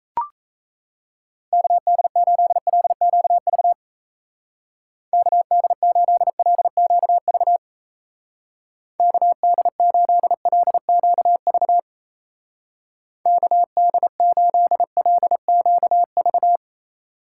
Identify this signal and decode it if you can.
Morse code, heard as a keyed tone, about 33 wpm, 700 Hz. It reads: KD8LQV KD8LQV KD8LQV KD8LQV
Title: Speed Racing - USA 2x3 Call Signs 28wpm